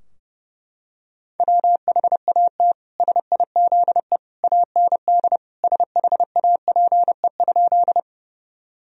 Morse code, heard as a keyed tone, about 30 wpm, 700 Hz. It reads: WHAT SIZE AND SHAPE?
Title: Sentences from Top 1000 Words - Review Part 3 - 30wpm